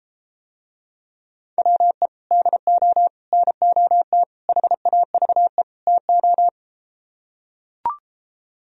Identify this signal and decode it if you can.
Morse code, heard as a keyed tone, about 33 wpm, 700 Hz. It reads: WE DO NOT HAVE TO